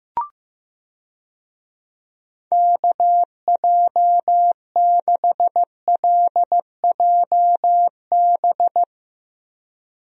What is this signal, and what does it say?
Morse code, heard as a keyed tone, about 15 wpm, 700 Hz. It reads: KJ6LJB
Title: USA Call Signs - 6 Characters - 15wpm